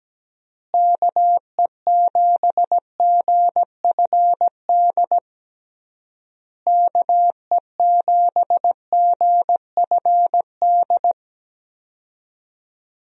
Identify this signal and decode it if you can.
Morse code, heard as a keyed tone, about 17 wpm, 700 Hz. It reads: KE7GFD KE7GFD
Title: USA Call Signs - 6 Characters - Repeated 3x - 17wpm